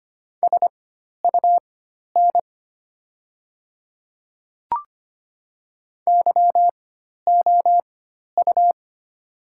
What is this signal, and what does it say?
Morse code, heard as a keyed tone, about 25 wpm, spaced as for 12 wpm, 700 Hz. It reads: SUN YOU
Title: 4x Character Spacing - Top 300 Words - 25wpm